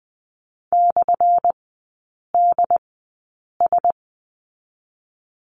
Morse code, heard as a keyed tone, about 20 wpm, spaced as for 8 wpm, 700 Hz. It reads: /DS
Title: 178 Lesson 29e - Slash - Three Characters 20wpm